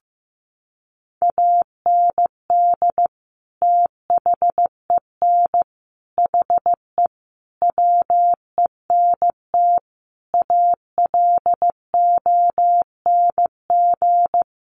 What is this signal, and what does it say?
Morse code, heard as a keyed tone, about 15 wpm, 700 Hz. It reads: AND THEN HE WENT ALONG
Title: Sentences from Top 1100 Words 15wpm